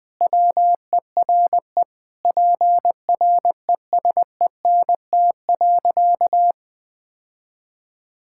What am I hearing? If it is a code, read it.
Morse code, heard as a keyed tone, about 20 wpm, 700 Hz. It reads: WERE PRESENT.
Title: Sentences from Top 1000 Words - Review Part 5 - 20wpm